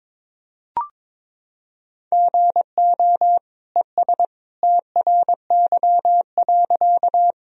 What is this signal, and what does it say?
Morse code, heard as a keyed tone, about 22 wpm, 700 Hz. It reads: GO ES TRY.
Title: Sentences from Top 200 Words and Abbreviations 22wpm